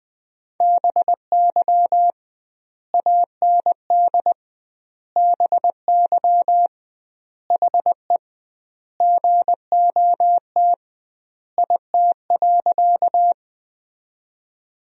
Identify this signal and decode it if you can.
Morse code, heard as a keyed tone, about 20 wpm, 700 Hz. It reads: BY AND BY HE GOT IT.
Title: 2x Word-Spacing - Sentences from Top 300 Words - 20wpm